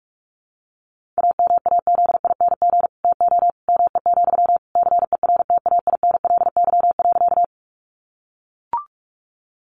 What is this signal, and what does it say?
Morse code, heard as a keyed tone, about 44 wpm, 700 Hz. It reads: AMAZING TO ME, CERTAINLY.
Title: Sentences from Top 1500 Words 45wpm